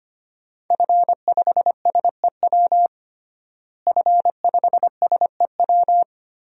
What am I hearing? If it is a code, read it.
Morse code, heard as a keyed tone, about 25 wpm, 700 Hz. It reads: F5SEW F5SEW
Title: DXCC-100 Call Signs - 5 Characters - Repeated 2x - 25wpm